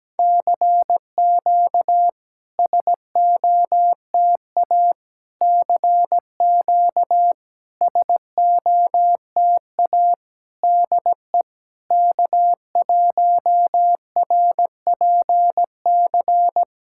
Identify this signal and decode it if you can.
Morse code, heard as a keyed tone, about 17 wpm, 700 Hz. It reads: CQ SOTA CQ SOTA DE K1RPC
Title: SOTA QSO Practice 17wpm